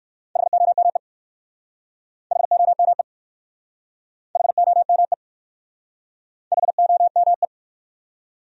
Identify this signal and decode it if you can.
Morse code, heard as a keyed tone, about 59 wpm, 700 Hz. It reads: HOME HOME HOME HOME